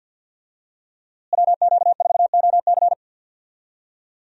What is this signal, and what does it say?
Morse code, heard as a keyed tone, about 50 wpm, 700 Hz. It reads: WQ4OC
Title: USA Call Signs - 5 Characters - Repeated 3x - 50wpm